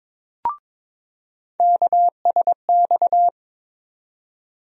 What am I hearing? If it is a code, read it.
Morse code, heard as a keyed tone, about 21 wpm, 700 Hz. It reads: KSX
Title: USA Call Sign Postfixes - 3 Letters 22wpm